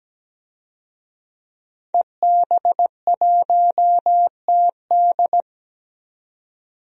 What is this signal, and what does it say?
Morse code, heard as a keyed tone, about 17 wpm, 700 Hz. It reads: EB1TD
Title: DXCC-100 Call Signs - 5 Characters - 17wpm